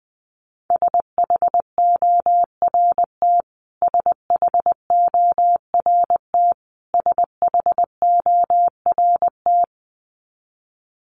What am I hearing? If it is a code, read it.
Morse code, heard as a keyed tone, about 20 wpm, 700 Hz. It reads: SHORT SHORT SHORT
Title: Top 500 Words Repeated 20wpm